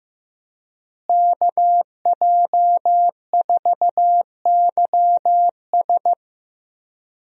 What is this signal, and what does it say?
Morse code, heard as a keyed tone, about 14 wpm, 700 Hz. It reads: KJ4YS
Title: USA Call Signs - 5 Characters - Repeated 3x - 15wpm